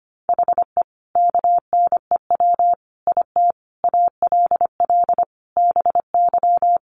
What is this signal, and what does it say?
Morse code, heard as a keyed tone, about 25 wpm, 700 Hz. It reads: HE KNEW IT ALL BY
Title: Sentences from Top 1200 Words 25wpm